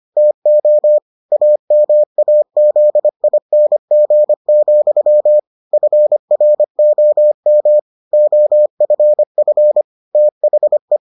Morse code, heard as a keyed tone, about 25 wpm, 600 Hz. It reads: TO AMAZING, FROM OFF THE